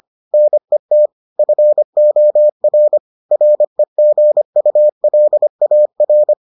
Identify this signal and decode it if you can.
Morse code, heard as a keyed tone, about 25 wpm, 600 Hz. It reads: NET FOR REGULAR